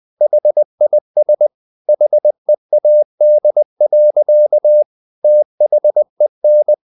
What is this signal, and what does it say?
Morse code, heard as a keyed tone, about 20 wpm, 600 Hz. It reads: HIS HEAD. THEN
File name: the old man and the sea part 2 20wpm